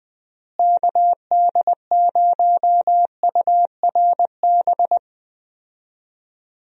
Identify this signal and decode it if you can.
Morse code, heard as a keyed tone, about 20 wpm, 700 Hz. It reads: KD0URB
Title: USA Call Signs - 6 Characters - Repeated 3x - 20wpm